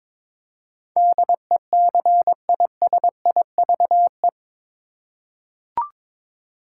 Morse code, heard as a keyed tone, about 22 wpm, 700 Hz. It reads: DECISIVE
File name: practice / 22wpm / Eight Letter Words - 22wpm